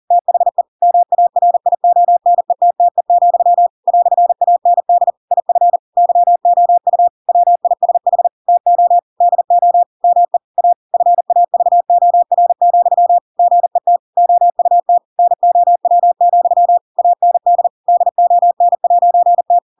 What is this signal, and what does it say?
Morse code, heard as a keyed tone, about 40 wpm, 700 Hz. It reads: THE MARIONETTE, "AND IF YOU WISH TO DO ME A FAVOR, GET OUT NOW, AND DON'T
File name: Pinocchio / Pinocchio ch4 40 40 ik1hge